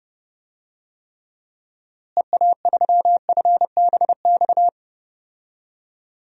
Morse code, heard as a keyed tone, about 30 wpm, 700 Hz. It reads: EA3FBX
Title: DXCC-100 Call Signs - 6 Characters - Repeated 2x - 30wpm